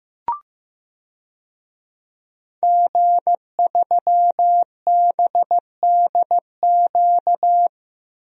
Morse code, heard as a keyed tone, about 15 wpm, 700 Hz. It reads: G3BDQ
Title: DXCC-100 Call Signs - 5 Characters - 15wpm